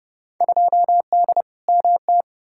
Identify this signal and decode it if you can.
Morse code, heard as a keyed tone, about 30 wpm, 700 Hz. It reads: 2D MT
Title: Field Day Exchanges Encore - 30wpm